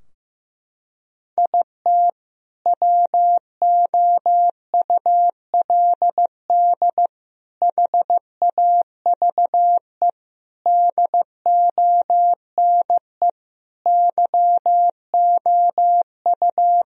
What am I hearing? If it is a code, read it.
Morse code, heard as a keyed tone, about 15 wpm, 700 Hz. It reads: IT WOULD HAVE DONE YOU
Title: Sentences from Top 1000 Words - Review Part 2 - 15wpm